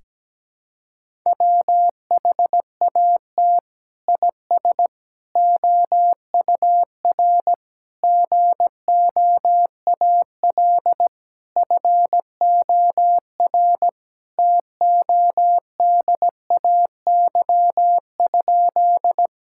Morse code, heard as a keyed tone, about 17 wpm, 700 Hz. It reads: WHAT IS OUR GOAL FOR TODAY?
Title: Sentences from Top 1000 Words - Review Part 5 - 17wpm